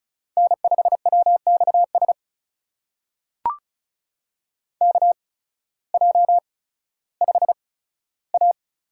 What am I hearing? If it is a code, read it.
Morse code, heard as a keyed tone, about 36 wpm, spaced as for 16 wpm, 700 Hz. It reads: N5WXS KJ5A
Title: USA Call Signs - 5 Characters - 8x Character Spacing - 35wpm